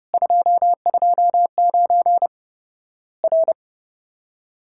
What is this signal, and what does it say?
Morse code, heard as a keyed tone, about 30 wpm, 700 Hz. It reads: 229 R
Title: Worked All Europe DX Contest - QTC Exchange - 30wpm